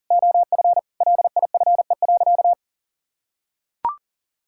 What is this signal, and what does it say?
Morse code, heard as a keyed tone, about 40 wpm, 700 Hz. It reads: OF LIFE.
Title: Sentences from Top 1000 Words - Review Part 6 - 40wpm